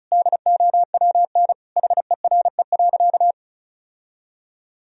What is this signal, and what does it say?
Morse code, heard as a keyed tone, about 34 wpm, 700 Hz. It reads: DOWN HERE.
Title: Sentences from Top 1000 Words - Review Part 2 - 35wpm